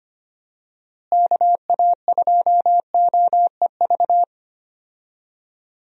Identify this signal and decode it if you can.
Morse code, heard as a keyed tone, about 25 wpm, 700 Hz. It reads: KA2OEV
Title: USA Call Signs - 6 Characters - Repeated 2x - 25wpm